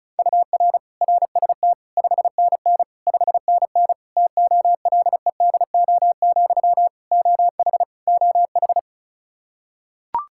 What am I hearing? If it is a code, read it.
Morse code, heard as a keyed tone, about 36 wpm, 700 Hz. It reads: UR RST 5NN 5NN TOLEDO, OH OH
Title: Practice QSO - Exchange 1 - Beginning - 35wpm